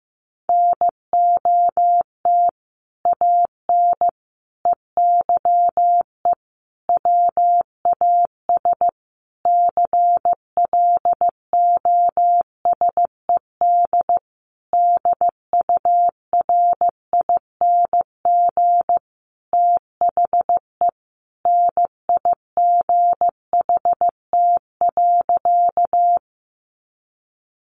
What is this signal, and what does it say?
Morse code, heard as a keyed tone, about 15 wpm, 700 Hz. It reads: NOT AN EYE WAS CLOSED DURING THE NIGHT.
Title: Sentences from Top 1400 Words 15wpm